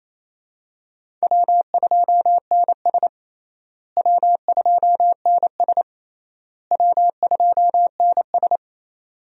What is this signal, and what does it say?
Morse code, heard as a keyed tone, about 28 wpm, 700 Hz. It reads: W2NS W2NS W2NS
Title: USA Call Signs - 4 Characters - Repeated 3x - 28wpm